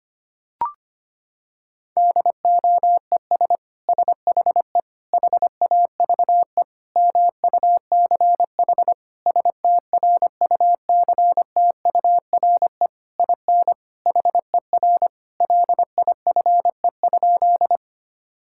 Morse code, heard as a keyed tone, about 25 wpm, 700 Hz. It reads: DOES SHE HAVE MUCH STRUCTURE IN HER LIFE?
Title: Sentences from Top 1000 Words - Review Part 6 - 25wpm